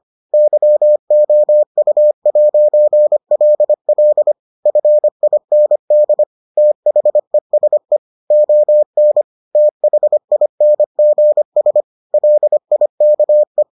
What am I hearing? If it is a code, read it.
Morse code, heard as a keyed tone, about 25 wpm, 600 Hz. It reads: YOU'LL FIND THESE ON THINGS LIKE